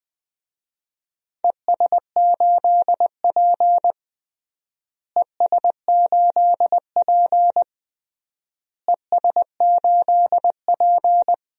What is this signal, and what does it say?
Morse code, heard as a keyed tone, about 21 wpm, 700 Hz. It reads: ES8P ES8P ES8P
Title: DXCC-100 Call Signs - 4 Characters - Repeated 3x - 20wpm